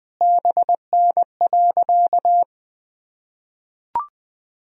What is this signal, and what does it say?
Morse code, heard as a keyed tone, about 20 wpm, 700 Hz. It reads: BN.
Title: Sentences from Top 300 Words and Abbreviations 20wpm